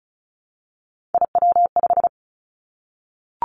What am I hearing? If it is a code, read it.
Morse code, heard as a keyed tone, about 36 wpm, 700 Hz. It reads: IW5
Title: Three Challenging Letter-Number Combinations 35wpm